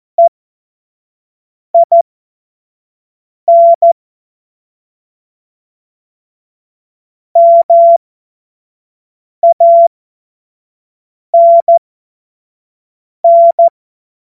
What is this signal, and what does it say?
Morse code, heard as a keyed tone, about 14 wpm, spaced as for 5 wpm, 680 Hz. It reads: EIN MANN
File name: Morsekurs / Witze / Witz 04. April 2023 25 F70